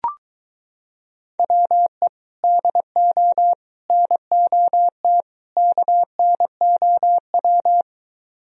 Morse code, heard as a keyed tone, about 23 wpm, 700 Hz. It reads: WE DO NOT KNOW